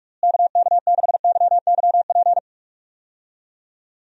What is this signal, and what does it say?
Morse code, heard as a keyed tone, about 46 wpm, 700 Hz. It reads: KK6YYP
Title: USA Call Signs - 6 Characters - Repeated 2x - 45wpm